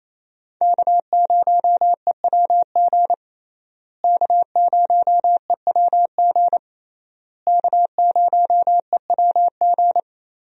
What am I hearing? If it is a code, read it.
Morse code, heard as a keyed tone, about 28 wpm, 700 Hz. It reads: K0EWG K0EWG K0EWG
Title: USA Call Signs - 5 Characters - Repeated 3x - 28wpm